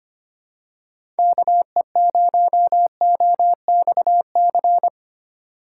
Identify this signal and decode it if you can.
Morse code, heard as a keyed tone, about 25 wpm, 700 Hz. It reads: KE0OXC
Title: USA Call Signs - 6 Characters - 3x Character Spacing - 25wpm